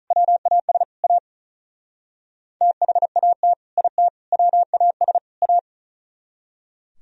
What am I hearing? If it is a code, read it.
Morse code, heard as a keyed tone, about 45 wpm, 700 Hz. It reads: WAS A THAT IT WAS A